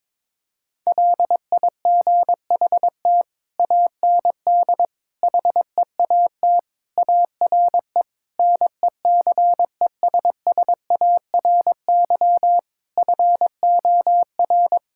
Morse code, heard as a keyed tone, about 22 wpm, 700 Hz. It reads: LIGHT AND HEAT ARE NECESSARY FOR
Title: Sentences from Top 1000 Words - Review Part 5 - 22wpm